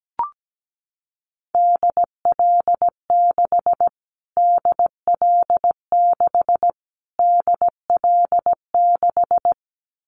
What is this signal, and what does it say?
Morse code, heard as a keyed tone, about 17 wpm, 700 Hz. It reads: DL6 DL6 DL6
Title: Callsign Prefixes for Top 25 DXCC - Repeated 3x - 17wpm